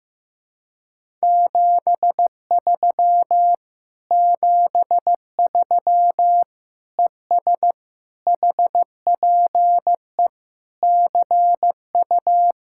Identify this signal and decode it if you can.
Morse code, heard as a keyed tone, about 15 wpm, 700 Hz. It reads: 73 73 ES HPE CU
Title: Speed-Racing Common QSO Phrases Encore - 15wpm